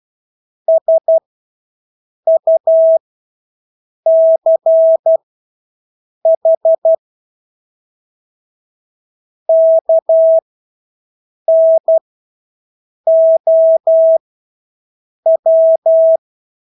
Morse code, heard as a keyed tone, about 12 wpm, spaced as for 6 wpm, 650 Hz. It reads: SUCH KNOW
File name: Engelske ord 12-6 wpm